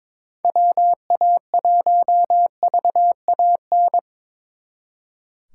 Morse code, heard as a keyed tone, about 25 wpm, 700 Hz. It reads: WA1VAN
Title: Speed Racing - USA 2x3 Call Signs 22wpm